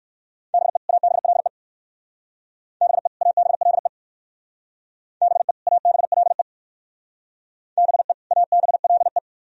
Morse code, heard as a keyed tone, about 59 wpm, 700 Hz. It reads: BE ABLE BE ABLE BE ABLE BE ABLE